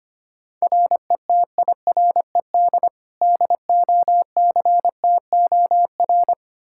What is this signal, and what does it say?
Morse code, heard as a keyed tone, about 25 wpm, 700 Hz. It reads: RETIRED DOCTOR
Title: Practice QSO - Exchange 3 - More Info - 25wpm